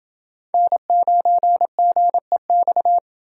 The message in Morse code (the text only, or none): N9GEX